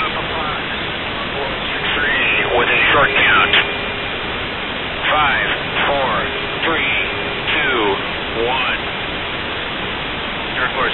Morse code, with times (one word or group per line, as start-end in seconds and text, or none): none